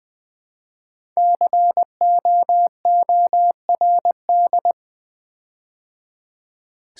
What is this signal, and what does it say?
Morse code, heard as a keyed tone, about 20 wpm, 700 Hz. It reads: COORD
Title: Build A Word - 5 Letters At A Time - 12-Letter Words - 20wpm